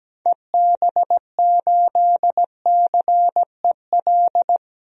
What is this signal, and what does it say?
Morse code, heard as a keyed tone, about 17 wpm, 700 Hz. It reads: EB8CEL
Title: DXCC-100 Call Signs - 6 Characters - 17wpm